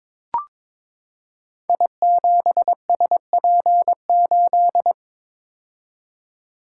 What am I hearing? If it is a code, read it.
Morse code, heard as a keyed tone, about 22 wpm, 700 Hz. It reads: I7SP8
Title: Code Groups - 5 Letter:Numbers - 22wpm